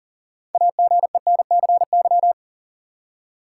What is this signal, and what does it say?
Morse code, heard as a keyed tone, about 40 wpm, 700 Hz. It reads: AGENCY
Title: Six Letter Words - 40wpm